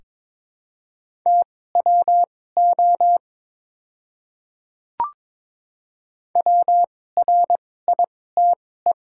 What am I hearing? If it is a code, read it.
Morse code, heard as a keyed tone, about 21 wpm, spaced as for 16 wpm, 700 Hz. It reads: TWO WRITE
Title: 2x Character Spacing - Top 200 Words - 22wpm